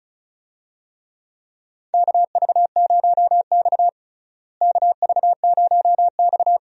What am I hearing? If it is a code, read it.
Morse code, heard as a keyed tone, about 34 wpm, 700 Hz. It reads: KV0X KV0X
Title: USA Call Signs - 4 Characters - Repeated 2x - 35wpm